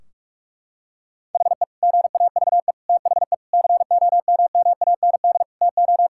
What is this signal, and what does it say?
Morse code, heard as a keyed tone, about 45 wpm, 700 Hz. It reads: HE GAVE THE COMMAND TO
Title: Sentences from Top 1600 Words 45wpm